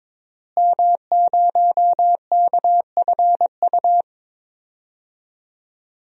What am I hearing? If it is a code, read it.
Morse code, heard as a keyed tone, about 21 wpm, 700 Hz. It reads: M0KFU
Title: DXCC-100 Call Signs - 5 Characters - Repeated 2x - 22wpm